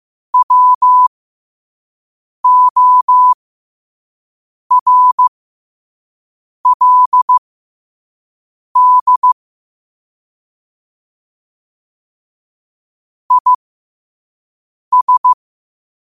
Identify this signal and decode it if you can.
Morse code, heard as a keyed tone, about 15 wpm, spaced as for 5 wpm, 1000 Hz. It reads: WORLD IS
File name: qotdfiles / 5wpm